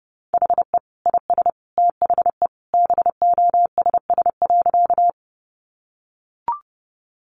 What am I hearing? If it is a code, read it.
Morse code, heard as a keyed tone, about 30 wpm, 700 Hz. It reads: HE IS THE BOSS.
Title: Sentences from Top 700 Words 30wpm